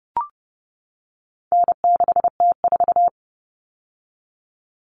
Morse code, heard as a keyed tone, about 30 wpm, 700 Hz. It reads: N6T4
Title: All Cut Numbers - 4-Digit 30wpm